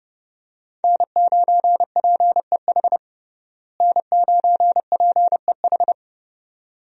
Morse code, heard as a keyed tone, about 32 wpm, 700 Hz. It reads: N9PEH N9PEH
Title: USA Call Signs - 5 Characters - Repeated 2x - 30wpm